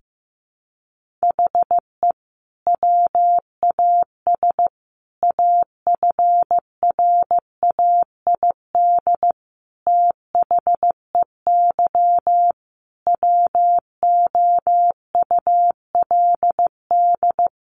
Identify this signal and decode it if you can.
Morse code, heard as a keyed tone, about 15 wpm, 700 Hz. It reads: HE WAS AFRAID THEY WOULD
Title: Sentences from Top 1500 Words 15wpm